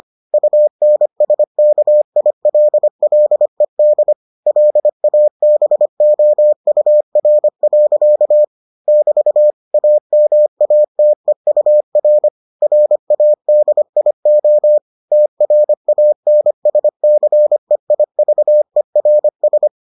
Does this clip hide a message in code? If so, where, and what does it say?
Yes: Morse code from 0.3 to 19.7 s, heard as a keyed tone, about 25 wpm, 600 Hz. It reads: UNSKILLED LABOUR. <BT> AMATEUR RADIO TRANSCEIVERS